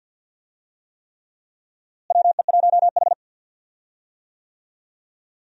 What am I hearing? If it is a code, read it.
Morse code, heard as a keyed tone, about 50 wpm, 700 Hz. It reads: WE1H
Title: USA Call Signs - 4 Characters - Repeated 3x - 50wpm